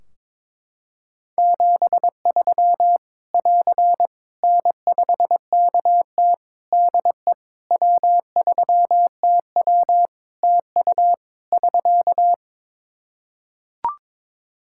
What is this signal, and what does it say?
Morse code, heard as a keyed tone, about 22 wpm, 700 Hz. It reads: 73 <AR> N5KT DE W3TW TU <SK>
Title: Practice QSO - Exchange 4 - Ending - 22wpm